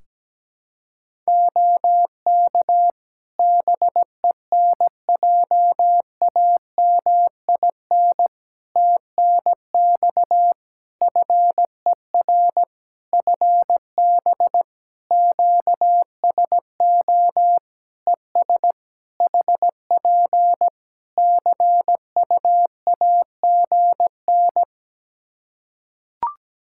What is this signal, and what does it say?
Morse code, heard as a keyed tone, about 17 wpm, 700 Hz. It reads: OK BENJAMIN TNX FER FB QSO ES HP CUAGN
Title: Practice QSO - Exchange 4 - Ending - 17wpm